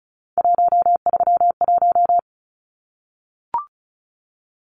Morse code, heard as a keyed tone, about 34 wpm, 700 Hz. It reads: 131
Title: Numbers - 3-Digit 35wpm